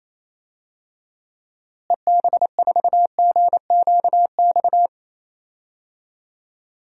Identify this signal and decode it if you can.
Morse code, heard as a keyed tone, about 27 wpm, 700 Hz. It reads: EB4GQX
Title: DXCC-100 Call Signs - 6 Characters - Repeated 2x - 28wpm